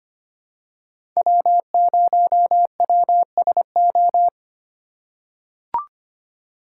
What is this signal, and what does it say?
Morse code, heard as a keyed tone, about 25 wpm, 700 Hz. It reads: W0WSO
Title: USA Call Signs - 5 Characters - 2x Character Spacing - 25wpm